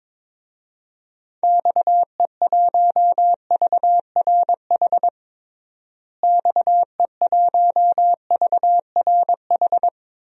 Morse code, heard as a keyed tone, about 22 wpm, 700 Hz. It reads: XE1VRH XE1VRH
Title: DXCC-100 Call Signs - 6 Characters - Repeated 3x - 22wpm